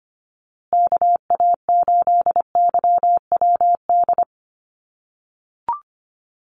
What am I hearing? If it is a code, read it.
Morse code, heard as a keyed tone, about 25 wpm, 700 Hz. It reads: KA8YWD
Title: USA Call Signs Repeated 2x 25wpm